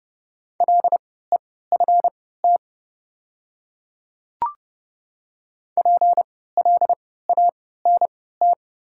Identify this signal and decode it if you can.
Morse code, heard as a keyed tone, about 30 wpm, spaced as for 17 wpm, 700 Hz. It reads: LEFT PLANT
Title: 3x Character Spacing - Top 300 Words - 30wpm